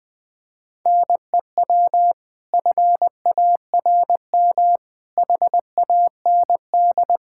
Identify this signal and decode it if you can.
Morse code, heard as a keyed tone, about 21 wpm, 700 Hz. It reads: NEW FARM HAND
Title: Pangram Challenge v2 - 20wpm